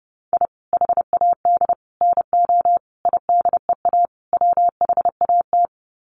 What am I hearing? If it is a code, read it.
Morse code, heard as a keyed tone, about 30 wpm, 700 Hz. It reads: I HAD NO IDEA WHAT
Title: Sets of 5 Words Encore 30wpm